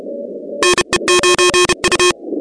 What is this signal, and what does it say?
Morse code, heard as a keyed tone, about 32 wpm, 2500 Hz, with noise. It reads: NE9U
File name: contests / 2019-cw-wpx-kt5j-k5pi- -2330 0-16